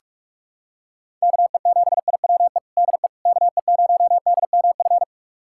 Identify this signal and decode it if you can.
Morse code, heard as a keyed tone, about 46 wpm, 700 Hz. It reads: KE7IWE DE KE0DMF
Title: USA Callsign DE Callsign 45wpm